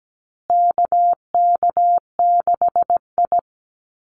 Morse code, heard as a keyed tone, about 17 wpm, 700 Hz. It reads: KK6I
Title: Call Signs Repeated 2x - 17wpm